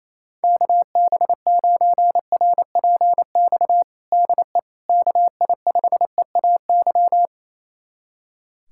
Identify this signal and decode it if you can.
Morse code, heard as a keyed tone, about 28 wpm, 700 Hz. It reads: KB9RPX DE KI5EAY
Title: USA Callsign DE Callsign - 6 Characters - 28wpm